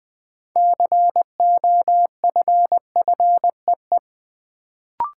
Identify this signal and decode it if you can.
Morse code, heard as a keyed tone, about 20 wpm, 700 Hz. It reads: COFFEE
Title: Build A Word - 2 Letters At A Time - 6-Letter Words - 20wpm